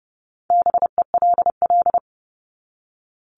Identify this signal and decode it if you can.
Morse code, heard as a keyed tone, about 30 wpm, 700 Hz. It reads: BELL
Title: Top 1751-2000 Words 30wpm